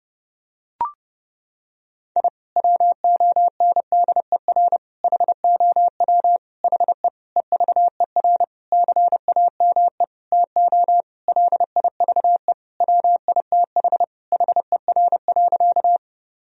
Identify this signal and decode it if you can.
Morse code, heard as a keyed tone, about 30 wpm, 700 Hz. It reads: I WONDER HOW HE EVER CAME TO LIVE WITH HER.
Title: Sentences from Top 1000 Words - Review Part 3 - 30wpm